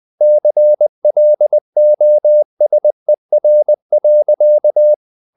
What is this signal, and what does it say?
Morse code, heard as a keyed tone, about 20 wpm, 600 Hz. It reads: CLOSER.
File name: the old man and the sea part 2 20wpm